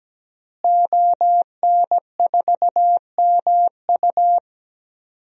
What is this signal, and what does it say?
Morse code, heard as a keyed tone, about 17 wpm, 700 Hz. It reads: ON4MU
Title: DXCC-100 Call Signs - 5 Characters - Repeated 3x - 17wpm